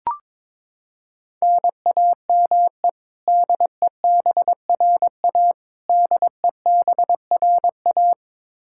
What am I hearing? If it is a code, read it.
Morse code, heard as a keyed tone, about 21 wpm, 700 Hz. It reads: NAME DEBRA DEBRA
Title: Practice QSO - Exchange 1 - Beginning - 22wpm